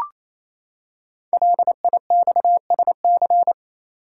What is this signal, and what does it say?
Morse code, heard as a keyed tone, about 27 wpm, 700 Hz. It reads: LIXSC
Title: Code Groups - 5 Letters - 28wpm